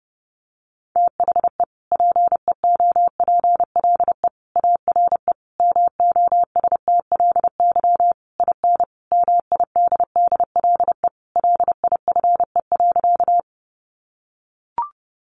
Morse code, heard as a keyed tone, about 30 wpm, 700 Hz. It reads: THE PEOPLE ARE MOSTLY IN MIDDLE LIFE.
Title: Sentences from Top 1300 Words 30wpm